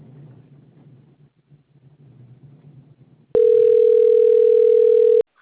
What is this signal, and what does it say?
Ringback tone (North America), 440 + 480 Hz.
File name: rg-1600-unknown- - - .49664